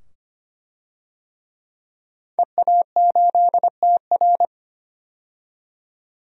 Morse code, heard as a keyed tone, about 25 wpm, 700 Hz. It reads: EA8TR
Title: DXCC-100 Call Signs - 5 Characters - 25wpm